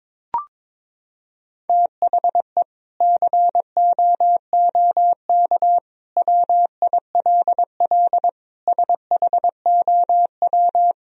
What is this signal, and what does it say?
Morse code, heard as a keyed tone, about 22 wpm, 700 Hz. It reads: THE COOK WILL SHOW
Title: Sentences from Top 1000 Words - Review Part 4 - 22wpm